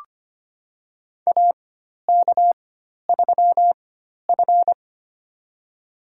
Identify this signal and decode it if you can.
Morse code, heard as a keyed tone, about 25 wpm, spaced as for 12 wpm, 700 Hz. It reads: AK3F
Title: USA Call Signs - 4 Characters - 4x Character Spacing - 25wpm